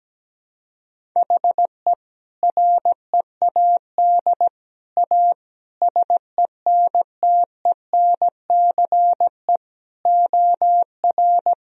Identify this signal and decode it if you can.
Morse code, heard as a keyed tone, about 17 wpm, 700 Hz. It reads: HE READ A SENTENCE OR
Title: Sentences from Top 1000 Words - Review Part 1 - 17wpm